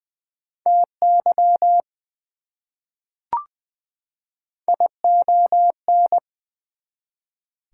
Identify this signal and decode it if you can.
Morse code, heard as a keyed tone, about 20 wpm, 700 Hz. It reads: TY ION
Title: Prefixes and Suffixes 20wpm